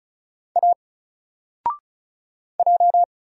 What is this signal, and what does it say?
Morse code, heard as a keyed tone, about 36 wpm, 700 Hz. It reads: A J
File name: Single-Letter-Number-35wpm